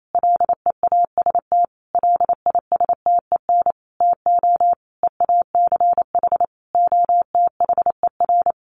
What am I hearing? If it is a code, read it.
Morse code, heard as a keyed tone, about 28 wpm, 700 Hz. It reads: LEAST LISTEN TO EACH OTHER
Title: Sentences from Top 1300 Words 28wpm